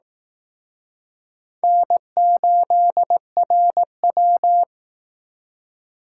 Morse code, heard as a keyed tone, about 18 wpm, 700 Hz. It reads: N8RW
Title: Speed Racing - USA Call Signs 15wpm